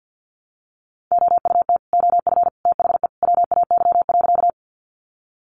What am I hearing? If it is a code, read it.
Morse code, heard as a keyed tone, about 50 wpm, 700 Hz. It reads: OUT OF THE WAY.
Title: Sentences from Top 100 Words 50wpm